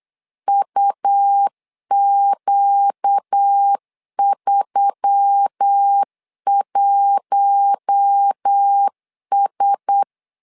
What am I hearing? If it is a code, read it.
Morse code, heard as a keyed tone, about 8 wpm, 800 Hz. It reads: UQ31S
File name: LEZIONI / 4 Q1S3 8wpm 3037